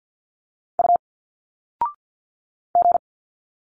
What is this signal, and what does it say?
Morse code, heard as a keyed tone, about 51 wpm, 700 Hz. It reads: U G
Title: Single Letters 50wpm